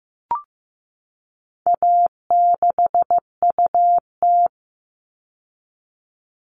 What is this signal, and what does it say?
Morse code, heard as a keyed tone, about 15 wpm, 700 Hz. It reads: A6UT
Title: All Cut Numbers - 4-Digit 15wpm